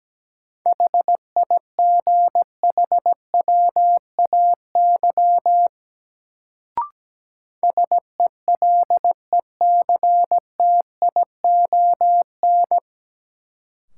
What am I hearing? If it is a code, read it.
Morse code, heard as a keyed tone, about 17 wpm, 700 Hz. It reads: HIGHWAY SELECTION
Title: Sentences from Top 1400 Words - New Word Drill - 17wpm